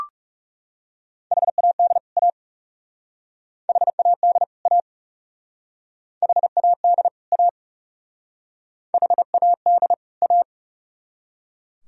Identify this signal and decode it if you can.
Morse code, heard as a keyed tone, about 37 wpm, 700 Hz. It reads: HAD A HAD A HAD A HAD A
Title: Speed-Racing - Sets of 2 Words 30wpm